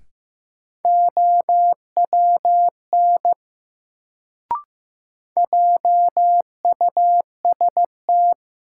Morse code, heard as a keyed tone, about 15 wpm, 700 Hz. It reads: OWN JUST
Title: Sets of 2 Words - New Word Drill - 15wpm